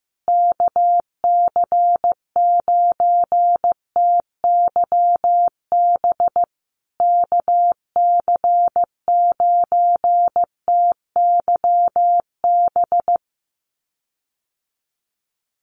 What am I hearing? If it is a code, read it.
Morse code, heard as a keyed tone, about 15 wpm, 700 Hz. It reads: KC9TYB KC9TYB
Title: USA Call Signs Repeated 3x 15wpm